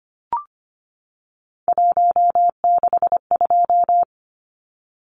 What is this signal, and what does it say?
Morse code, heard as a keyed tone, about 25 wpm, 700 Hz. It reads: 162
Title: Numbers - 3-Digit 25wpm